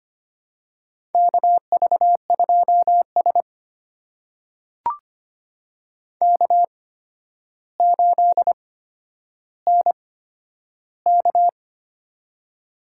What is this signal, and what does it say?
Morse code, heard as a keyed tone, about 25 wpm, spaced as for 11 wpm, 700 Hz. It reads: KV2S K8NK
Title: USA Call Signs - 4 Characters - 8x Character Spacing - 25wpm